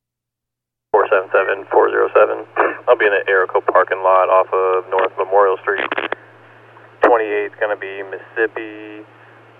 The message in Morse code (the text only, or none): none